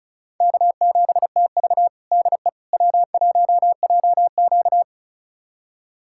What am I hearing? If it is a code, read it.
Morse code, heard as a keyed tone, about 36 wpm, 700 Hz. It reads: K7TV DE W1JQ
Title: USA Callsign DE Callsign - 4 Characters - 35wpm